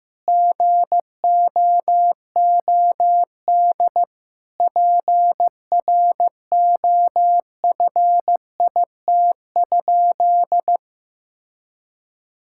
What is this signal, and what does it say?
Morse code, heard as a keyed tone, about 15 wpm, 700 Hz. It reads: GOOD PROFIT?
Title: Sentences from Top 1000 Words - Review Part 5 - 15wpm